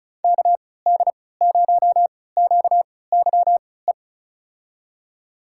Morse code, heard as a keyed tone, about 34 wpm, spaced as for 20 wpm, 700 Hz. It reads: KD0QYE
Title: USA Call Signs - 6 Characters - 3x Character Spacing - 35wpm